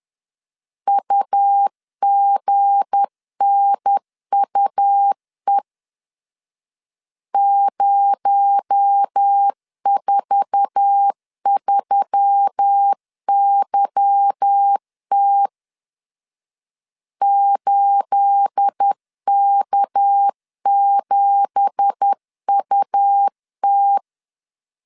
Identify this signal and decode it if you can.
Morse code, heard as a keyed tone, about 10 wpm, 800 Hz. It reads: UGNUE 043YT 8K7UT
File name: LEZIONI / 9 2867 10wpm 2733